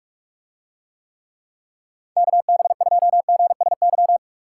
Morse code, heard as a keyed tone, about 44 wpm, 700 Hz. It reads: KB2GSY